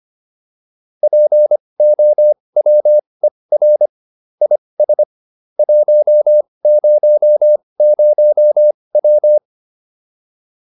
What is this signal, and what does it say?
Morse code, heard as a keyed tone, about 25 wpm, spaced as for 20 wpm, 600 Hz. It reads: POWER IS 100W